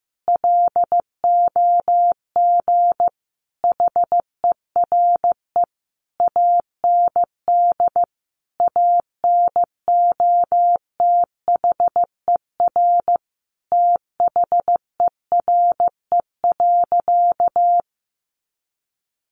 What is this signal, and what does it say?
Morse code, heard as a keyed tone, about 14 wpm, 700 Hz. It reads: LOG HERE AND ANOTHER THERE.
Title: Sentences from Top 1400 Words 15wpm